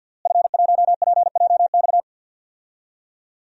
Morse code, heard as a keyed tone, about 46 wpm, 700 Hz. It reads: F1PJX
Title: DXCC-100 Call Signs - 5 Characters - Repeated 3x - 50wpm